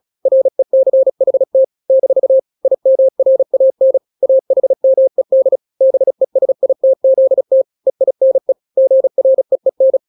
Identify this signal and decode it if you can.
Morse code, heard as a keyed tone, about 36 wpm, 530 Hz. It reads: RECHT - IMRAN AHMED BESITZT EINE GREEN
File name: daily / 35wpm